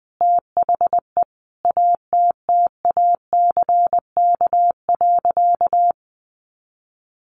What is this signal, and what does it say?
Morse code, heard as a keyed tone, about 20 wpm, 700 Hz. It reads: THE ATTACK.
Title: Sentences from Top 1500 Words 20wpm